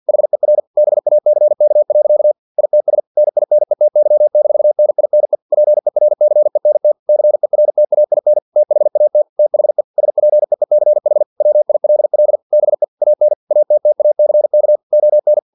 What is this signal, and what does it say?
Morse code, heard as a keyed tone, about 50 wpm, 600 Hz. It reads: HER BACK. ITS NINETY-NINE PERCENT CERTAIN THAT THE SPEECH WILL BE AN ATTACK ON
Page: Never 49wpm